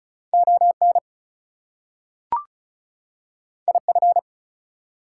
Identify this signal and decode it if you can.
Morse code, heard as a keyed tone, about 36 wpm, 700 Hz. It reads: ON IF
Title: Two Letter Words - 35wpm